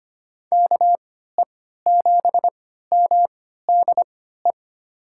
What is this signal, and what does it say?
Morse code, heard as a keyed tone, about 25 wpm, spaced as for 14 wpm, 700 Hz. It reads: KE7MDE
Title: USA Call Signs - 6 Characters - 3x Character Spacing - 25wpm